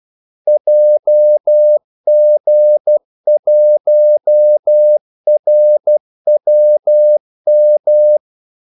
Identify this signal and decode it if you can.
Morse code, heard as a keyed tone, about 12 wpm, 600 Hz. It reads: JG1RWM
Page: 2 ekstra ordmellemrum mellem kaldesignalerne